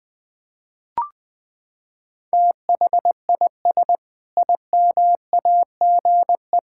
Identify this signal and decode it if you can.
Morse code, heard as a keyed tone, about 21 wpm, 700 Hz. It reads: THIS IMAGE
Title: Sentences with 5-Word Phrases - 20wpm